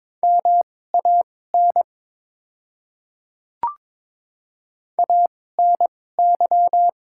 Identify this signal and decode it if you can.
Morse code, heard as a keyed tone, about 23 wpm, spaced as for 16 wpm, 690 Hz. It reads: MAN ANY
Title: 2x Character Spacing - Top 500 Words - 22wpm